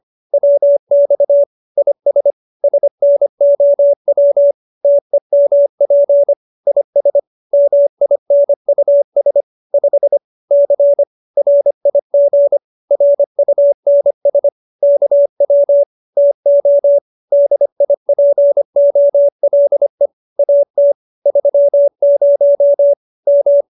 Morse code, heard as a keyed tone, about 25 wpm, 570 Hz. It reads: WX IS SNOW TEMP IS MINUS 5 C RIG RUNS KW TO DIPOLE AT 30 M
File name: QSO305 25